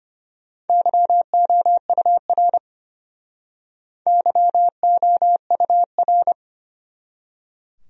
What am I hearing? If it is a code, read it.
Morse code, heard as a keyed tone, about 33 wpm, 700 Hz. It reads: YOUR YOUR